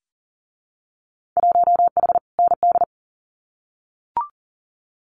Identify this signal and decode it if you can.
Morse code, heard as a keyed tone, about 40 wpm, 700 Hz. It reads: 1H ND
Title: Winter Field Day Exchanges 40wpm